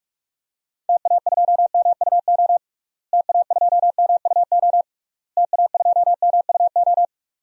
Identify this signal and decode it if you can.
Morse code, heard as a keyed tone, about 46 wpm, 700 Hz. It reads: TA2MUO TA2MUO TA2MUO
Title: DXCC-100 Call Signs - 6 Characters - Repeated 3x - 45wpm